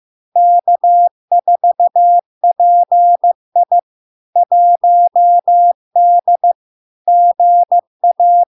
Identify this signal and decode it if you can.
Morse code, heard as a keyed tone, about 15 wpm, 700 Hz. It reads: K4PI 1D GA
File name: LARK-FD-2022-Log-in-CW-15wpm-700-Hz